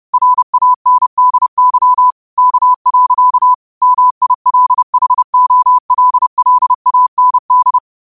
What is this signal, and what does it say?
Morse code, heard as a keyed tone, about 30 wpm, 1000 Hz. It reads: RANDY K. MILHOLLAND
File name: qotdfiles / 30wpm